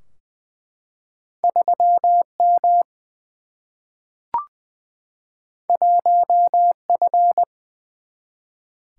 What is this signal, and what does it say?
Morse code, heard as a keyed tone, about 20 wpm, 700 Hz. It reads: 3 M 1 F
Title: Single-Letter Single-Number 20wpm